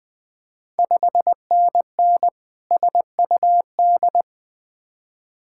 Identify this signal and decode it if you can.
Morse code, heard as a keyed tone, about 20 wpm, 700 Hz. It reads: 5NN SUD
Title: Ontario QSO Party Exchange 20wpm